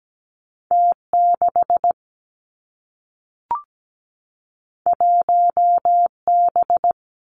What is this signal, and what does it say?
Morse code, heard as a keyed tone, about 17 wpm, 700 Hz. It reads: T6 1B
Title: Two Challenging Letter-Number Combinations 17wpm